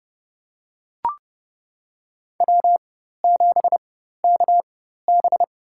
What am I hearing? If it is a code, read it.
Morse code, heard as a keyed tone, about 30 wpm, spaced as for 14 wpm, 700 Hz. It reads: W7KB
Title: USA Call Signs - 4x Character Spacing - 30wpm